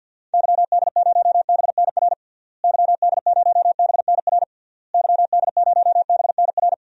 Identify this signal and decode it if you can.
Morse code, heard as a keyed tone, about 49 wpm, 700 Hz. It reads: YD0BNR YD0BNR YD0BNR
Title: DXCC-100 Call Signs - 6 Characters - Repeated 3x - 50wpm